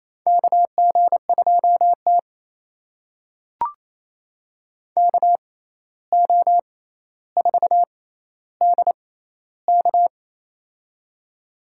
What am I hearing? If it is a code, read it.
Morse code, heard as a keyed tone, about 28 wpm, spaced as for 13 wpm, 700 Hz. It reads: KG2T KO4DK
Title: USA Call Signs - 6x Character Spacing - 28wpm